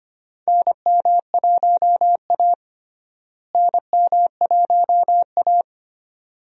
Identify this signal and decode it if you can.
Morse code, heard as a keyed tone, about 25 wpm, 700 Hz. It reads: NM1A NM1A
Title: USA Call Signs - 4 Characters - Repeated 2x - 25wpm